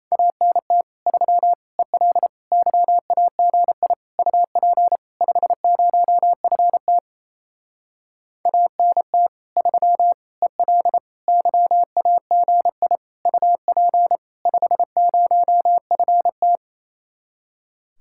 Morse code, heard as a keyed tone, about 31 wpm, 700 Hz. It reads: ANT 3 EL YAGI UP 50FT ANT 3 EL YAGI UP 50FT
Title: Speed-Racing Common QSO Phrases Encore - 28wpm